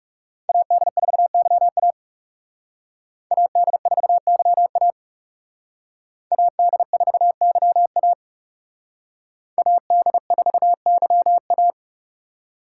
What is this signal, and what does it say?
Morse code, heard as a keyed tone, about 36 wpm, 700 Hz. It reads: AD4YA AD4YA AD4YA AD4YA
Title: Speed Racing - USA 2x2 Call Signs 30wpm